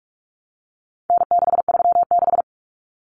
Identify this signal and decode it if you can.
Morse code, heard as a keyed tone, about 44 wpm, 700 Hz. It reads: N636
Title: Common Cut Numbers - 4-Digit 45wpm